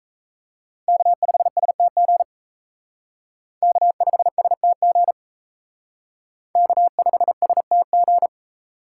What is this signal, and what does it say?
Morse code, heard as a keyed tone, about 36 wpm, 700 Hz. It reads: K5STG K5STG K5STG